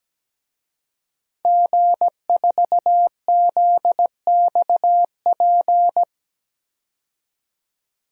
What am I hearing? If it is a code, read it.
Morse code, heard as a keyed tone, about 17 wpm, 700 Hz. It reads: G4ZXP
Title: DXCC-100 Call Signs - 5 Characters - 17wpm